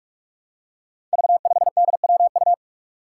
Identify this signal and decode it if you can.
Morse code, heard as a keyed tone, about 46 wpm, 700 Hz. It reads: V5DWU
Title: Five Challenging Letter-Number Combinations 45wpm